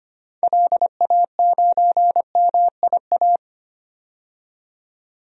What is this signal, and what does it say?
Morse code, heard as a keyed tone, about 25 wpm, 700 Hz. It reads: LA9MIA
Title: DXCC-100 Call Signs - 6 Characters - 25wpm